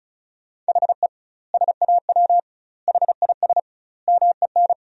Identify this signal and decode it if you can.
Morse code, heard as a keyed tone, about 34 wpm, 700 Hz. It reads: HE SAW HIS MEN
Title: 2x Word-Spacing - Sentences from Top 300 Words - 35wpm